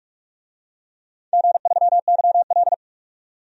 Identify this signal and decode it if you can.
Morse code, heard as a keyed tone, about 44 wpm, 700 Hz. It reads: G3YL
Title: DXCC-100 Call Signs - 4 Characters - 45wpm